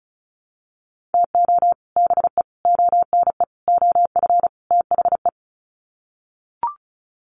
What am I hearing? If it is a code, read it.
Morse code, heard as a keyed tone, about 34 wpm, 700 Hz. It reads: TO BE ONE OF THE
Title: Sets of 5 Words 35wpm